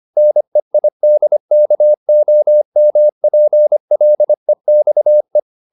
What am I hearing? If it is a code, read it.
Morse code, heard as a keyed tone, about 25 wpm, 600 Hz. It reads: NEIDKOMPLEXE